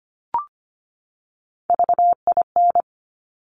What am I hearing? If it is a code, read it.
Morse code, heard as a keyed tone, about 25 wpm, 700 Hz. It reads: VIN
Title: Top Three Letter Combinations in Words 25wpm